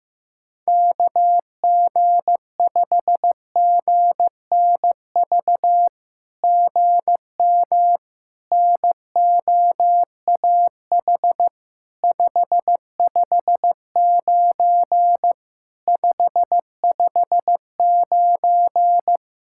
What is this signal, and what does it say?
Morse code, heard as a keyed tone, about 15 wpm, 700 Hz. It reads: KG5GNV GM NOAH 559 559
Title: SOTA QSO Practice 15wpm